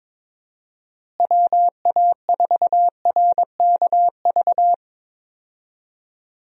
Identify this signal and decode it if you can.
Morse code, heard as a keyed tone, about 22 wpm, 700 Hz. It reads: WA4RKV
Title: USA Call Signs - 6 Characters - Repeated 3x - 22wpm